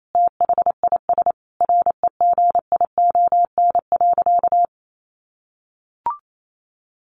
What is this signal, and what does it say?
Morse code, heard as a keyed tone, about 28 wpm, 700 Hz. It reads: THIS REGION.
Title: Sentences from Top 1500 Words 28wpm